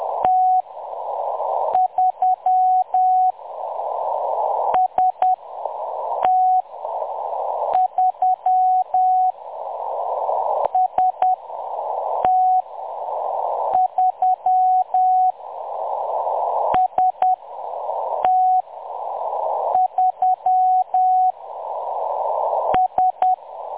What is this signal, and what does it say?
Morse code, heard as a keyed tone, about 10 wpm, spaced as for 6 wpm, 740 Hz. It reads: T3 ST3 ST3 ST3 S
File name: M23-ST3- -0858utc-10184khz